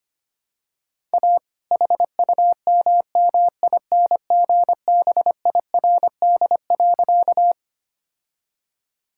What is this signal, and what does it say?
Morse code, heard as a keyed tone, about 25 wpm, 700 Hz. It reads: A HUMMINGBIRD.
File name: practice / 25wpm / All-Age Appropriate Jokes - 25wpm